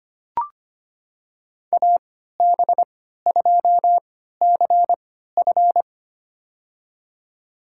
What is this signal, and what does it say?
Morse code, heard as a keyed tone, about 25 wpm, spaced as for 14 wpm, 700 Hz. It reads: AB2CF
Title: USA Call Signs - 5 Characters - 3x Character Spacing - 25wpm